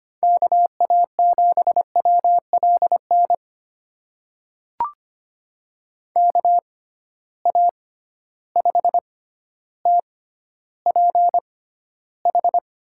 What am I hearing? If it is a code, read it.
Morse code, heard as a keyed tone, about 25 wpm, spaced as for 13 wpm, 700 Hz. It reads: KA7WLN KA5TPH
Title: USA Call Signs - 6 Characters - 6x Character Spacing - 25wpm